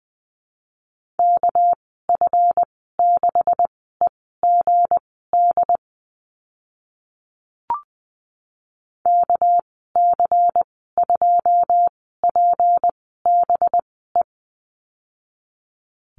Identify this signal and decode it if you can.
Morse code, heard as a keyed tone, about 20 wpm, spaced as for 14 wpm, 700 Hz. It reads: KF6EGD KC2PBE
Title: 219 Lesson 35g - Number 0 - Call Signs 20wpm